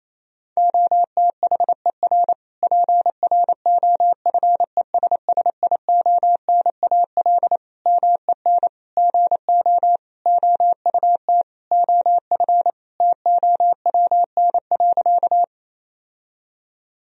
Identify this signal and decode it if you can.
Morse code, heard as a keyed tone, about 28 wpm, 700 Hz. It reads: OTHER PROFESSIONAL MEN GO OUT OF TOWN.
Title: Sentences from Top 1000 Words - Review Part 4 - 28wpm